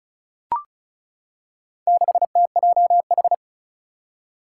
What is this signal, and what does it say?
Morse code, heard as a keyed tone, about 36 wpm, 700 Hz. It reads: 6TJH
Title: Four Challenging Letter-Number Combinations 35wpm